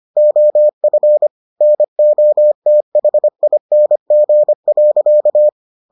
Morse code, heard as a keyed tone, about 25 wpm, 600 Hz. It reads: OF NOTHING.